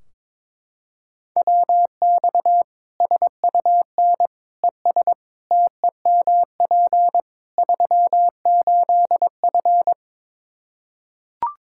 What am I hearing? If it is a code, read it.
Morse code, heard as a keyed tone, about 22 wpm, 700 Hz. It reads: WX SUN ES TEMP 38F
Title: Practice QSO - Exchange 2 - Info - 22wpm